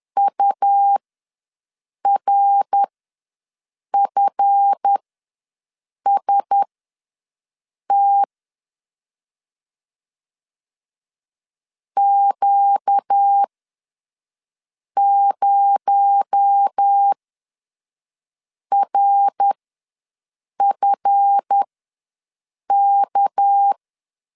Morse code, heard as a keyed tone, about 10 wpm, spaced as for 5 wpm, 800 Hz. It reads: URFST Q0RFK
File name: LEZIONI / 7 GF9Z / 7 GF ws 10wpm 3918